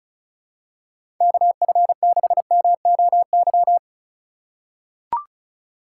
Morse code, heard as a keyed tone, about 36 wpm, 700 Hz. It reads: KF6MOY
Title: USA Call Signs - 6 Characters - 2x Character Spacing - 35wpm